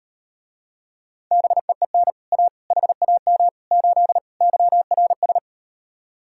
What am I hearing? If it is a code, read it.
Morse code, heard as a keyed tone, about 38 wpm, 700 Hz. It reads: BEEN A HAM 8 YRS